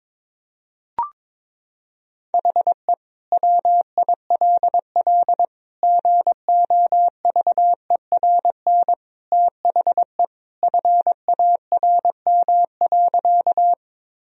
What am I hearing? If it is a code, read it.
Morse code, heard as a keyed tone, about 22 wpm, 700 Hz. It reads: HE WILL GOVERN THE FARM.
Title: Sentences from Top 1000 Words - Review Part 2 - 22wpm